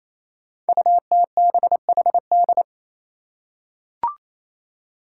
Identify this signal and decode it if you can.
Morse code, heard as a keyed tone, about 28 wpm, 700 Hz. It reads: UTBHD
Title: Five Challenging Letter-Number Combinations 28wpm